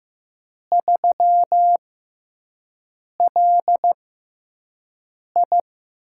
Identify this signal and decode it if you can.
Morse code, heard as a keyed tone, about 15 wpm, spaced as for 5 wpm, 700 Hz. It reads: 3LI
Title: USA Call Signs - 4 Characters - 6x Character Spacing - 15wpm